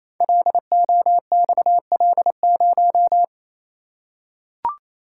Morse code, heard as a keyed tone, about 28 wpm, 700 Hz. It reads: LOXL0
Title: Code Groups - 5 Letter:Numbers - 28wpm